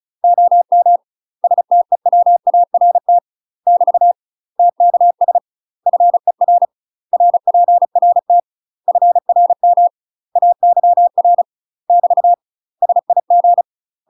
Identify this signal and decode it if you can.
Morse code, heard as a keyed tone, about 34 wpm, 700 Hz. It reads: OM STEWART <BT> TKS FER RPRT FRM AYR <BT> SIG
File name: UK MASTER QSOs TRAINING / 35WPM / UK MASTER QSOs 0014